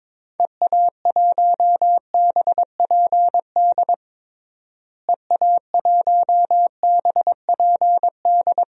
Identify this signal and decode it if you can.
Morse code, heard as a keyed tone, about 22 wpm, 700 Hz. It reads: EA1BPD EA1BPD
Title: DXCC-100 Call Signs - 6 Characters - Repeated 2x - 22wpm